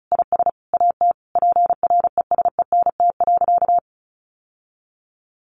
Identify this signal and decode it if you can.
Morse code, heard as a keyed tone, about 34 wpm, 700 Hz. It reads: IS AT PRESENT.
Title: Sentences from Top 1200 Words 35wpm